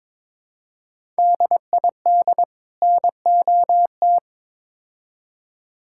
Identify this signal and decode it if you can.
Morse code, heard as a keyed tone, about 22 wpm, 700 Hz. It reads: DID NOT
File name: practice / 22wpm / Build A Phrase - 5-Word Phrases - 22wpm